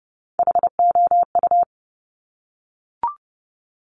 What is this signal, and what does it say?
Morse code, heard as a keyed tone, about 30 wpm, 700 Hz. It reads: HOU
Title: Top Three Letter Combinations in Words 30wpm